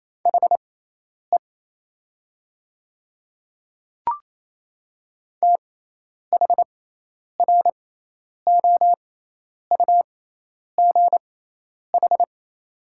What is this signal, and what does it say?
Morse code, heard as a keyed tone, about 28 wpm, spaced as for 10 wpm, 690 Hz. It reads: HE THROUGH
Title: 6x Character Spacing - Top 200 Words - 28wpm